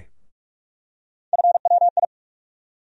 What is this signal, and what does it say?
Morse code, heard as a keyed tone, about 42 wpm, 700 Hz. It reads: FWI
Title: USA Call Sign Postfixes - 3 Letters 45wpm